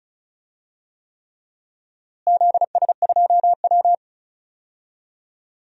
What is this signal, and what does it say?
Morse code, heard as a keyed tone, about 36 wpm, 700 Hz. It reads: ZS2W
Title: DXCC-100 Call Signs - 4 Characters - Repeated 3x - 35wpm